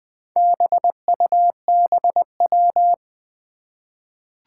Morse code, heard as a keyed tone, about 20 wpm, 700 Hz. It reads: BUBW
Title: Four Challenging Letter-Number Combinations 20wpm